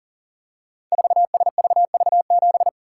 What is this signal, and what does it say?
Morse code, heard as a keyed tone, about 41 wpm, 690 Hz. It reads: 4SVV7
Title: Code Groups - 5 Letter:Numbers - 40wpm